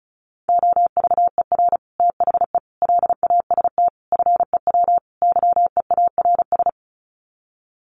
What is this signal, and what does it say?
Morse code, heard as a keyed tone, about 34 wpm, 700 Hz. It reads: OVER THE LAST FEW YEARS
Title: Sets of 5 Words Encore 35wpm